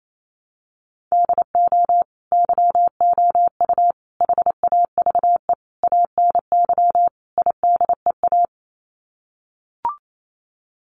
Morse code, heard as a keyed tone, about 28 wpm, 700 Hz. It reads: DO YOU HAVE ANY IDEA
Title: Sets of 5 Words Encore 28wpm